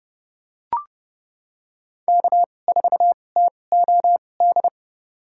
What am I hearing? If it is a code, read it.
Morse code, heard as a keyed tone, about 31 wpm, spaced as for 22 wpm, 700 Hz. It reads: K4TOD
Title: USA Call Signs - 5 Characters - 2x Character Spacing - 30wpm